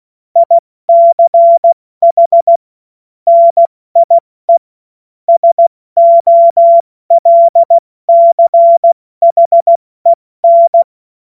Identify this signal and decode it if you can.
Morse code, heard as a keyed tone, about 16 wpm, 680 Hz. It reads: ICH NIE SOLCHEN
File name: Witz-15-Juli-2025-70-F80